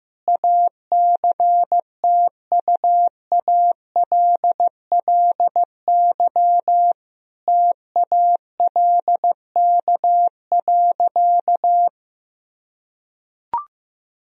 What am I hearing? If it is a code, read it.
Morse code, heard as a keyed tone, about 16 wpm, 700 Hz. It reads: ACTUALLY TALK.
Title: Sentences from Top 1000 Words - Review Part 4 - 15wpm